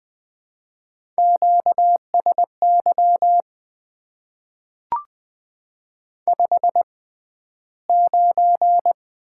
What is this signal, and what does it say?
Morse code, heard as a keyed tone, about 20 wpm, spaced as for 12 wpm, 700 Hz. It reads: QSY 59
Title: Top 90 QSO Elements - 6x Character Spacing - 20wpm